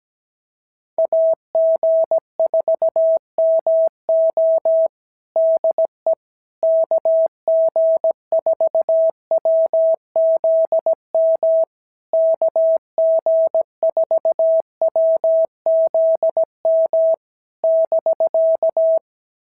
Morse code, heard as a keyed tone, about 17 wpm, 650 Hz. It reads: AG4MO DE KG4WZM KG4WZM <BK>
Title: SOTA QSO Practice 17wpm